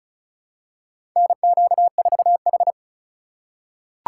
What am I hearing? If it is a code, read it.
Morse code, heard as a keyed tone, about 34 wpm, 700 Hz. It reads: NQ4H
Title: USA Call Signs - 4 Characters - 3x Character Spacing - 35wpm